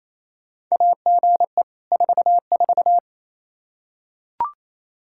Morse code, heard as a keyed tone, about 28 wpm, 700 Hz. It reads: AGE 44
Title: Practice QSO - Exchange 3 - More Info - 28wpm